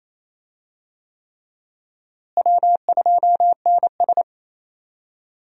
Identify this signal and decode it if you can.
Morse code, heard as a keyed tone, about 28 wpm, 700 Hz. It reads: W2NS
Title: USA Call Signs - 4 Characters - Repeated 3x - 28wpm